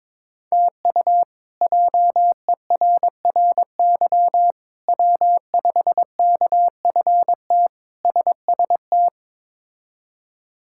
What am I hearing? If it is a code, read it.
Morse code, heard as a keyed tone, about 22 wpm, 700 Hz. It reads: TU JERRY W5KFT SST
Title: K1USN SST QSO Practice - 22wpm